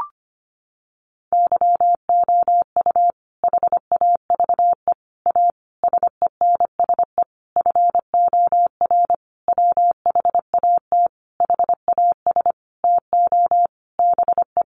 Morse code, heard as a keyed tone, about 25 wpm, 700 Hz. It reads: YOU HAVE A SENSE FOR WHAT HAS TO BE
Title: Sentences from Top 700 Words 25wpm